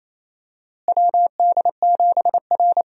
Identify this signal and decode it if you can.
Morse code, heard as a keyed tone, about 28 wpm, 700 Hz. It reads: WD7R
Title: USA Call Signs - 4 Characters - Repeated 3x - 28wpm